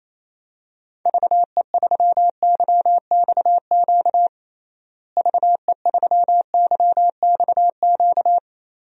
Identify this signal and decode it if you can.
Morse code, heard as a keyed tone, about 28 wpm, 700 Hz. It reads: VE3YXQ VE3YXQ
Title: DXCC-100 Call Signs - 6 Characters - Repeated 3x - 28wpm